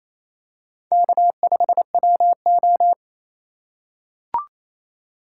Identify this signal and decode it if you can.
Morse code, heard as a keyed tone, about 28 wpm, 700 Hz. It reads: K5WO
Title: USA Call Signs - 4 Characters - 4x Character Spacing - 28wpm